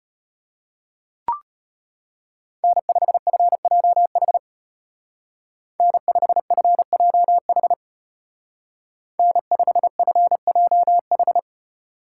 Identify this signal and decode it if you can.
Morse code, heard as a keyed tone, about 31 wpm, 700 Hz. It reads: N5FJH N5FJH N5FJH
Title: Speed Racing - USA 1x3 Call Signs 25wpm